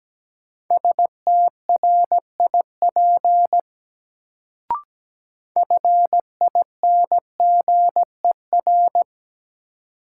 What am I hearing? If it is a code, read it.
Morse code, heard as a keyed tone, about 17 wpm, 700 Hz. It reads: STRIP FINGER
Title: Sentences from Top 1600 Words - New Word Drill - 17wpm